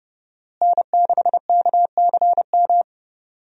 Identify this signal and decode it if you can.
Morse code, heard as a keyed tone, about 32 wpm, 700 Hz. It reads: N6KCM
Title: USA Call Signs - 5 Characters - Repeated 3x - 30wpm